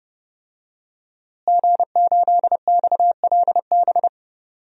Morse code, heard as a keyed tone, about 30 wpm, 700 Hz. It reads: G8XLB
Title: DXCC-100 Call Signs - 5 Characters - Repeated 3x - 30wpm